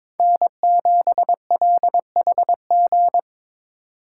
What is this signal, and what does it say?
Morse code, heard as a keyed tone, about 26 wpm, 700 Hz. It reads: N7LHG
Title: Speed Racing - USA 1x3 Call Signs 22wpm